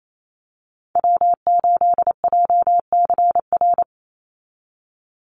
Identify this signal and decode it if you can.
Morse code, heard as a keyed tone, about 28 wpm, 700 Hz. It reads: W8JCR
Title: USA Call Signs 28wpm